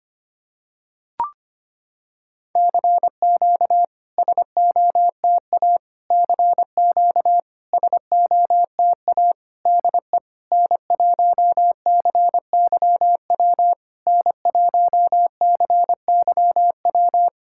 Morse code, heard as a keyed tone, about 25 wpm, 700 Hz. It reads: CQ SOTA CQ SOTA DE N1CYW N1CYW
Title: SOTA QSO Practice 25wpm